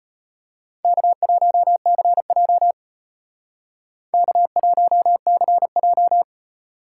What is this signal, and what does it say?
Morse code, heard as a keyed tone, about 32 wpm, 700 Hz. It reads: K1CJ K1CJ